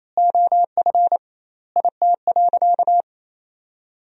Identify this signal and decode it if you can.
Morse code, heard as a keyed tone, about 27 wpm, 700 Hz. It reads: OF IT.
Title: 2x Word-Spacing - Sentences from Top 200 Words - 28wpm